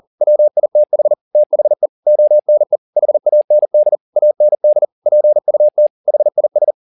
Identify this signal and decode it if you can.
Morse code, heard as a keyed tone, about 40 wpm, 600 Hz. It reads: WITH THE ONE HAND AND PUT HIS